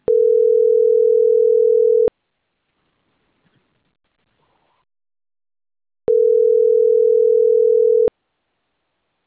Ringback tone (North America), 440 + 480 Hz.